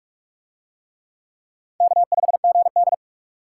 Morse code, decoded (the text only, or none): K5GD